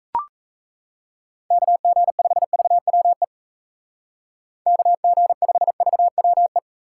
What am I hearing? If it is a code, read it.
Morse code, heard as a keyed tone, about 35 wpm, 700 Hz. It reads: KG5VWE KG5VWE